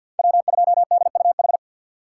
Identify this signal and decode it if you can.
Morse code, heard as a keyed tone, about 54 wpm, 700 Hz. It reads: W2DUH
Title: USA Call Signs - 5 Characters - 8x Character Spacing - 50wpm